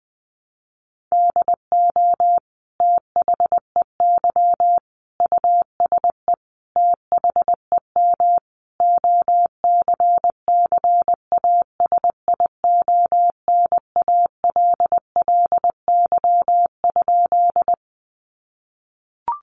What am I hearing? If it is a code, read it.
Morse code, heard as a keyed tone, about 20 wpm, 700 Hz. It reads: DO THEY USE THEM OCCASIONALLY?
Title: Sentences from Top 1300 Words 20wpm